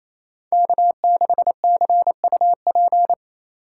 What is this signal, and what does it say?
Morse code, heard as a keyed tone, about 33 wpm, 700 Hz. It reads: K6CUP
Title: Speed Racing - USA 1x3 Call Signs 28wpm